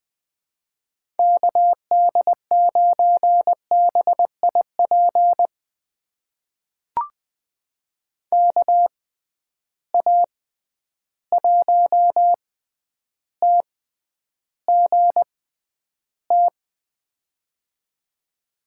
Morse code, heard as a keyed tone, about 20 wpm, spaced as for 10 wpm, 700 Hz. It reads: KD9BIP KA1TGT
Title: USA Call Signs - 6 Characters - 6x Character Spacing - 20wpm